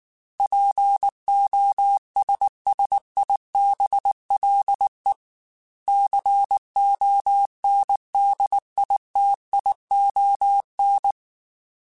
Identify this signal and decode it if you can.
Morse code, heard as a keyed tone, about 19 wpm, 800 Hz. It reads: POSSIBLE CONDITION
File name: Technical Subjects / Technical Subjects-mp3s / AND Gate @18